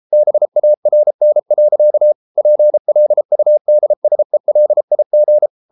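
Morse code, heard as a keyed tone, about 33 wpm, 600 Hz. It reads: BARN. PLUDSELIG
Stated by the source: Morse code at 33 wpm, QRQ.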